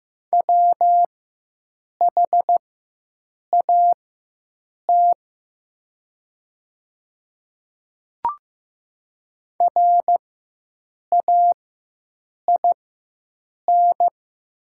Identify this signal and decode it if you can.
Morse code, heard as a keyed tone, about 15 wpm, spaced as for 7 wpm, 700 Hz. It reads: WHAT RAIN
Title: 4x Character Spacing - Top 300 Words - 15wpm